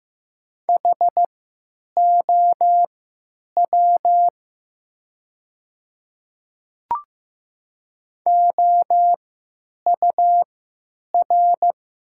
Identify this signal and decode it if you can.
Morse code, heard as a keyed tone, about 15 wpm, spaced as for 9 wpm, 700 Hz. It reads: HOW OUR
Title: 3x Character Spacing - Top 200 Words - 15wpm